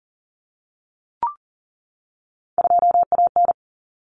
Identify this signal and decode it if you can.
Morse code, heard as a keyed tone, about 40 wpm, 700 Hz. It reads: 2AN
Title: Common Cut Numbers - 3-Digit 40wpm